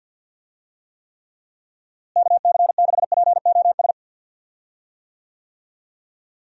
Morse code, heard as a keyed tone, about 51 wpm, 700 Hz. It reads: KC6POS
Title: USA Call Signs - 6 Characters - Repeated 2x - 50wpm